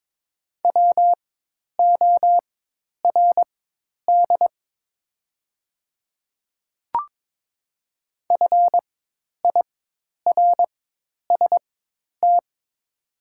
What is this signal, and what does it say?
Morse code, heard as a keyed tone, about 22 wpm, spaced as for 10 wpm, 700 Hz. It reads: WORD FIRST
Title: 4x Character Spacing - Top 500 Words - 22wpm